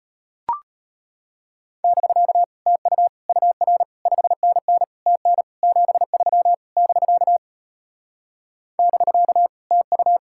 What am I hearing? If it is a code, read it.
Morse code, heard as a keyed tone, about 33 wpm, 700 Hz. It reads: <BK> TU UR 5NN TN 73 <BK> <BK> TU